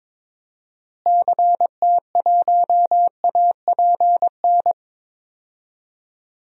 Morse code, heard as a keyed tone, about 22 wpm, 700 Hz. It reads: CT1APN
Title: DXCC-100 Call Signs - 6 Characters - Repeated 2x - 22wpm